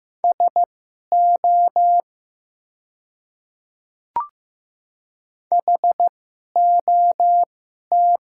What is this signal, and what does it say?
Morse code, heard as a keyed tone, about 15 wpm, spaced as for 11 wpm, 700 Hz. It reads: SO HOT
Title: 2x Character Spacing - Top 100 Words - 15wpm